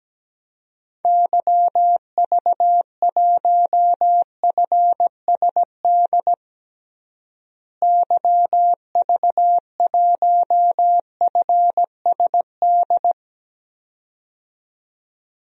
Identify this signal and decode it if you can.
Morse code, heard as a keyed tone, about 17 wpm, 700 Hz. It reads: YV1FSD YV1FSD
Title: DXCC-100 Call Signs - 6 Characters - Repeated 2x - 17wpm